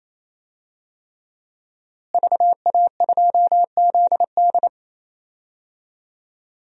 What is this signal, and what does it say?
Morse code, heard as a keyed tone, about 27 wpm, 700 Hz. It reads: VA2ZD
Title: DXCC-100 Call Signs - 5 Characters - Repeated 3x - 28wpm